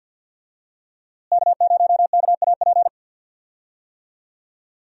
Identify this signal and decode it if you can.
Morse code, heard as a keyed tone, about 51 wpm, 700 Hz. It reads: K0KAP
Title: USA Call Signs - 5 Characters - Repeated 3x - 50wpm